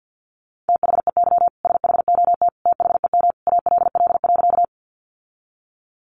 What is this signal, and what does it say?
Morse code, heard as a keyed tone, about 50 wpm, 700 Hz. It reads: THEY SHOT THEM ALL.
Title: Sentences from Top 1100 Words 50wpm